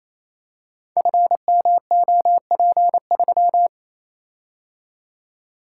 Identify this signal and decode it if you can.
Morse code, heard as a keyed tone, about 28 wpm, 700 Hz. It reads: FMOP3
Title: Code Groups - 5 Letter:Numbers - 28wpm